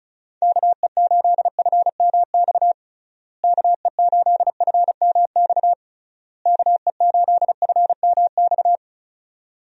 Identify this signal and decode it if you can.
Morse code, heard as a keyed tone, about 36 wpm, 700 Hz. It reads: KE8FMX KE8FMX KE8FMX
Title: USA Call Signs - 6 Characters - Repeated 3x - 35wpm